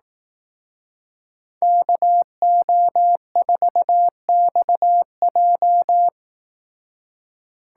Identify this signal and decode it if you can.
Morse code, heard as a keyed tone, about 18 wpm, 700 Hz. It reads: KO4XJ
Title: Speed Racing - USA 2x2 Call Signs 15wpm